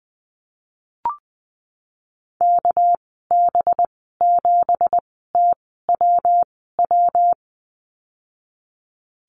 Morse code, heard as a keyed tone, about 20 wpm, spaced as for 14 wpm, 700 Hz. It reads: KB7TWW
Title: 238 Lesson 38g - Letter q - Call Signs 20wpm